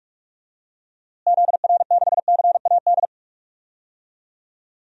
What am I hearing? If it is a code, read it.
Morse code, heard as a keyed tone, about 44 wpm, 700 Hz. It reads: ZR6CAD
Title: DXCC-100 Call Signs - 6 Characters - Repeated 2x - 45wpm